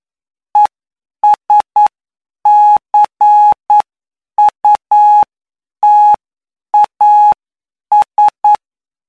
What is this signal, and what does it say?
Morse code, heard as a keyed tone, about 9 wpm, spaced as for 8 wpm, 840 Hz. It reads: ESCUTAS
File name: cw / py2lcd / 20-PY2LCD-Li��o 29